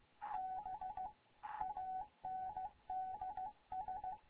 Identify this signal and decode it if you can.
Morse code, heard as a keyed tone, about 15 wpm, 750 Hz. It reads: BANDS
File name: Morze-HA2MN / hangp / 2 11 GYAKORLAT